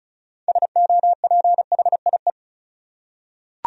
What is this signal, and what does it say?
Morse code, heard as a keyed tone, about 36 wpm, 700 Hz. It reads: SOPHIE
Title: USA Common Names - 35wpm